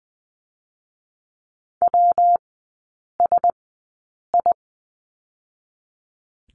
Morse code, heard as a keyed tone, about 19 wpm, spaced as for 8 wpm, 700 Hz. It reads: WSI
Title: 121 Lesson 20e - Number 3 - Three Characters 20wpm